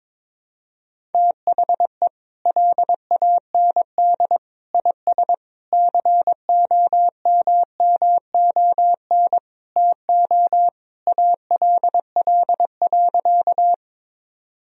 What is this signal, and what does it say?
Morse code, heard as a keyed tone, about 22 wpm, 700 Hz. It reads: THE LAND IS COMMON TO ALL.
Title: Sentences from Top 1000 Words - Review Part 3 - 22wpm